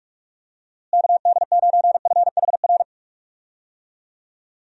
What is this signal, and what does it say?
Morse code, heard as a keyed tone, about 42 wpm, 700 Hz. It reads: KD9FHR
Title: USA Call Signs - 6 Characters - Repeated 2x - 45wpm